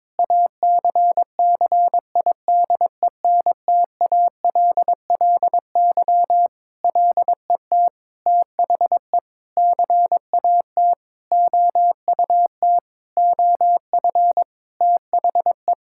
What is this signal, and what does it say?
Morse code, heard as a keyed tone, about 21 wpm, 700 Hz. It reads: ACCIDENTALLY LET THE CAT OUT OF THE
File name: Sentences with 7-Word Phrases - 22wpm